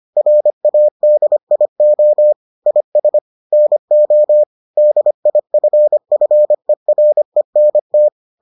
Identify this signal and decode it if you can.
Morse code, heard as a keyed tone, about 25 wpm, 600 Hz. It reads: RADIO IS NO DIFFERENT